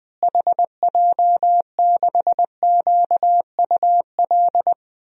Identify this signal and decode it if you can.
Morse code, heard as a keyed tone, about 20 wpm, 700 Hz. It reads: HJ6QUL
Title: DXCC-100 Call Signs - 6 Characters - Repeated 3x - 20wpm